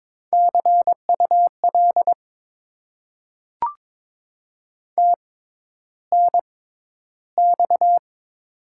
Morse code, heard as a keyed tone, about 22 wpm, spaced as for 11 wpm, 700 Hz. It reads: CUL TNX
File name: Top 90 QSO Elements - 6x Character Spacing - 22wpm